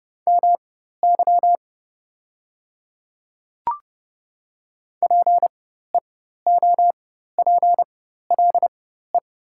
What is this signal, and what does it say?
Morse code, heard as a keyed tone, about 31 wpm, spaced as for 14 wpm, 690 Hz. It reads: MY PEOPLE
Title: 4x Character Spacing - Top 100 Words - 30wpm